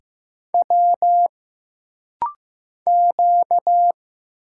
Morse code, heard as a keyed tone, about 15 wpm, 700 Hz. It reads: W Q
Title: Single Letter-Number Alt - Rapid-Fire 15wpm